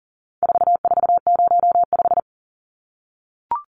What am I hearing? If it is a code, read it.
Morse code, heard as a keyed tone, about 40 wpm, 700 Hz. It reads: 4405
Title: Numbers - 4-Digit 40wpm